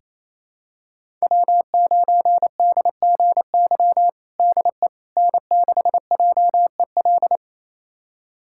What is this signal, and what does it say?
Morse code, heard as a keyed tone, about 27 wpm, 700 Hz. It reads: W9DGY DE N6JEL
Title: USA Callsign DE Callsign - 5 Characters - 28wpm